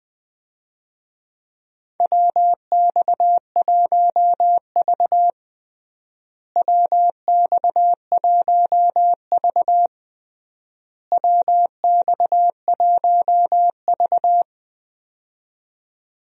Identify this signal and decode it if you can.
Morse code, heard as a keyed tone, about 20 wpm, 700 Hz. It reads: WX1V WX1V WX1V
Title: USA Call Signs - 4 Characters - Repeated 3x - 20wpm